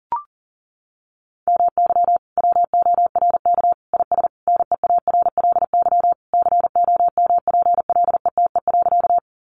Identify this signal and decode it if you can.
Morse code, heard as a keyed tone, about 37 wpm, 700 Hz. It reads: MY WORK IS NEARLY COMPLETE.
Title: Sentences from Top 1300 Words 40wpm